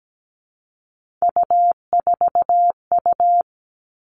Morse code, heard as a keyed tone, about 17 wpm, 700 Hz. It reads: U4U
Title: Three Challenging Letter-Number Combinations 17wpm